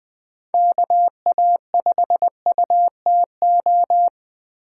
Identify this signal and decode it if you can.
Morse code, heard as a keyed tone, about 20 wpm, 700 Hz. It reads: KA5UTO
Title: USA Call Signs - 6 Characters - 3x Character Spacing - 20wpm